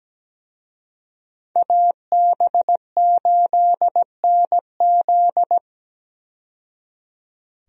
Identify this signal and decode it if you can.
Morse code, heard as a keyed tone, about 17 wpm, 700 Hz. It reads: AB8NZ
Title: USA Call Signs - 5 Characters - 17wpm